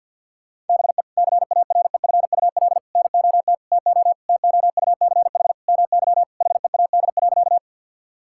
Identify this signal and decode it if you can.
Morse code, heard as a keyed tone, about 50 wpm, 700 Hz. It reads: BE CAREFUL NOT TO TOUCH MY HEAD.
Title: Sentences from Top 1600 Words 50wpm